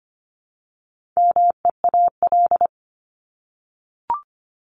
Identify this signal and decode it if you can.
Morse code, heard as a keyed tone, about 25 wpm, 700 Hz. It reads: MEAL
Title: Top 1751-2000 Words 25wpm